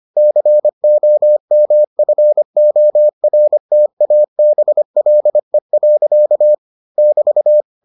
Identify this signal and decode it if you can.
Morse code, heard as a keyed tone, about 25 wpm, 600 Hz. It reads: COMFORTABLE. <BT>